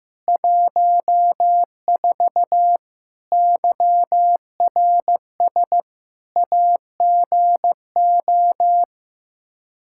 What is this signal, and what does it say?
Morse code, heard as a keyed tone, about 16 wpm, 700 Hz. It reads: 14 YRS AGO
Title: Practice QSO - Exchange 3 - More Info - 15wpm